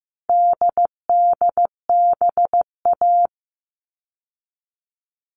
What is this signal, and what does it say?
Morse code, heard as a keyed tone, about 15 wpm, 700 Hz. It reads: DDBA
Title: All Cut Numbers - 4-Digit 15wpm